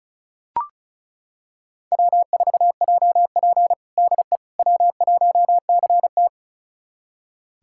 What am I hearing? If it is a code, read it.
Morse code, heard as a keyed tone, about 34 wpm, 700 Hz. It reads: W4JP DE W1CT
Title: USA Callsign DE Callsign - 4 Characters - 35wpm